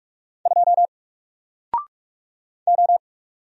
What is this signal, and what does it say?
Morse code, heard as a keyed tone, about 44 wpm, 700 Hz. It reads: 2 O
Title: Single Character (Letter:Number:Punctuation:Prosign) 45wpm